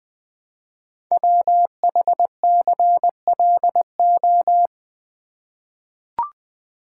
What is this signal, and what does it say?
Morse code, heard as a keyed tone, about 20 wpm, 700 Hz. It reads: WHCLO
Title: Code Groups - 5 Letter:Numbers - 20wpm